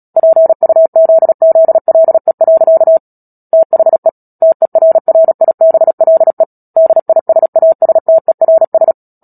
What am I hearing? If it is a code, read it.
Morse code, heard as a keyed tone, about 37 wpm, 650 Hz. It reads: PUZZLE. THE TERRIBLE DISASTERS
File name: Book Two / 10-Chapter 10 - Epilogue - 36 WPM